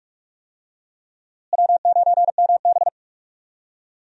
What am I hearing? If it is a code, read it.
Morse code, heard as a keyed tone, about 46 wpm, 700 Hz. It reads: W9MB
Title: USA Call Signs - 4 Characters - Repeated 3x - 45wpm